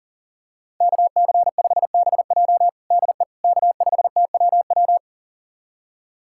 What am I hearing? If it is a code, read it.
Morse code, heard as a keyed tone, about 37 wpm, 700 Hz. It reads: KC5BJ DE K5TWW
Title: USA Callsign DE Callsign - 5 Characters - 40wpm